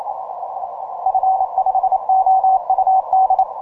Morse code, heard as a keyed tone, about 27 wpm, 750 Hz. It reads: F5OUD
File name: simple-file-list / VR2XAN 10MCW